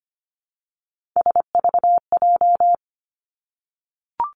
Morse code, heard as a keyed tone, about 25 wpm, 700 Hz. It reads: SVJ
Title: Three Challenging Letter-Number Combinations 25wpm